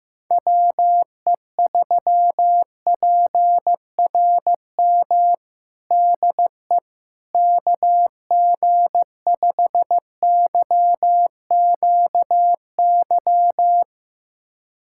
Morse code, heard as a keyed tone, about 15 wpm, 700 Hz. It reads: WE3PRM DE KG5YQY
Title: USA Callsign DE Callsign - 6 Characters - 15wpm